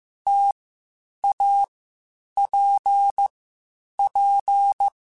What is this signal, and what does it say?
Morse code, heard as a keyed tone, about 15 wpm, 800 Hz. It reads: T A P P
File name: Difficult Combos / Difficult Combos-mp3s / T AOEI @13